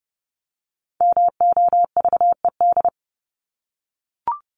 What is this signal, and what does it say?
Morse code, heard as a keyed tone, about 30 wpm, 700 Hz. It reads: MOVED
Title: ED Suffix - Top 100 Words with Base Words 30wpm